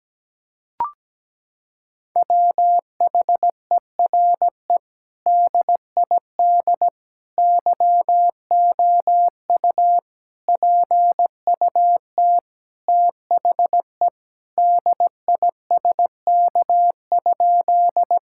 Morse code, heard as a keyed tone, about 17 wpm, 700 Hz. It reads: WHERE DID YOU PUT THE DISK?
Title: Sentences from Top 1000 Words - Review Part 6 - 17wpm